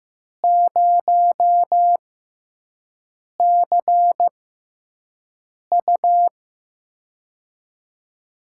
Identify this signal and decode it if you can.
Morse code, heard as a keyed tone, about 15 wpm, spaced as for 5 wpm, 700 Hz. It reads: 0CU
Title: USA Call Signs - 4 Characters - 6x Character Spacing - 15wpm